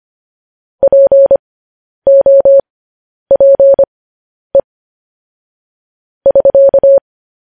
Morse code, heard as a keyed tone, about 25 wpm, spaced as for 10 wpm, 550 Hz. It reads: POPE <SK>
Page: News Headlines in Morse Code at 10 WPM